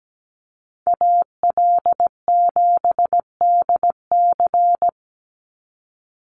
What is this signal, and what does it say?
Morse code, heard as a keyed tone, about 17 wpm, 700 Hz. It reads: AL7DC
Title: USA Call Signs 17wpm